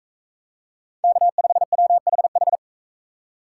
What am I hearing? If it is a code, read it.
Morse code, heard as a keyed tone, about 41 wpm, 700 Hz. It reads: K5WHH